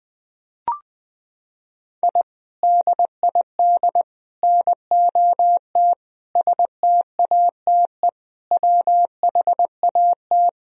Morse code, heard as a keyed tone, about 20 wpm, 700 Hz. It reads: I DID NOT STATE WHAT
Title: Sentences from Top 1000 Words - Review Part 2 - 20wpm